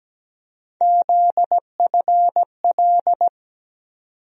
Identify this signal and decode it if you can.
Morse code, heard as a keyed tone, about 17 wpm, 700 Hz. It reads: ZFL
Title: USA Call Sign Postfixes - 3 Letters 17wpm